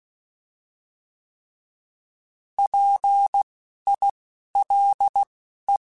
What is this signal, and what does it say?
Morse code, heard as a keyed tone, about 16 wpm, spaced as for 11 wpm, 800 Hz. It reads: PILE
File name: Alphabetical / Alphabetical mp3s / Ps @10